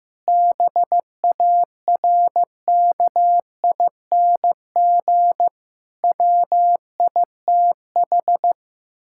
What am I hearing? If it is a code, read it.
Morse code, heard as a keyed tone, about 15 wpm, 700 Hz. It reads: BARKING WITH
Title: Pangram Challenge v2 - 15wpm